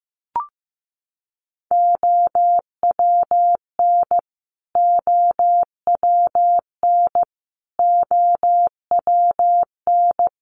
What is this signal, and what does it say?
Morse code, heard as a keyed tone, about 15 wpm, 700 Hz. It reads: OWN OWN OWN
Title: Top 500 Words Repeated 15wpm